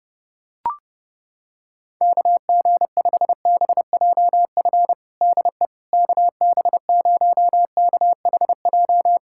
Morse code, heard as a keyed tone, about 30 wpm, 700 Hz. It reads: KG5BJF DE KB0KHJ
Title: USA Callsign DE Callsign - 6 Characters - 30wpm